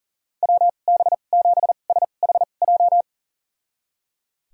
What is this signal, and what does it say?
Morse code, heard as a keyed tone, about 37 wpm, spaced as for 28 wpm, 700 Hz. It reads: WB7SHJ
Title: USA Call Signs - 2x Character Spacing - 40wpm